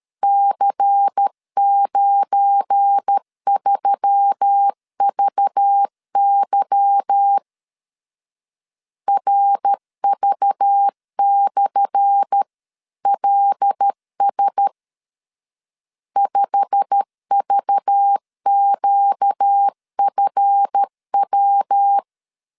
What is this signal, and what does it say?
Morse code, heard as a keyed tone, about 12 wpm, 800 Hz. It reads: C93VY RV/LS 5VQFW
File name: LEZIONI / 10 SIGNS / 10C DOT 12wpm 2451